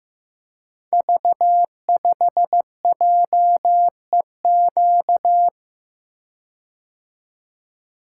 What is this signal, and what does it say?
Morse code, heard as a keyed tone, about 15 wpm, 700 Hz. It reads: V5JEQ
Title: Code Groups - 5 Letter:Numbers - 15wpm